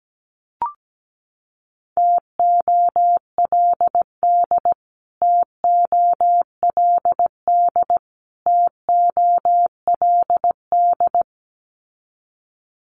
Morse code, heard as a keyed tone, about 17 wpm, 700 Hz. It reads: TOLD TOLD TOLD
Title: Top 500 Words Repeated 17wpm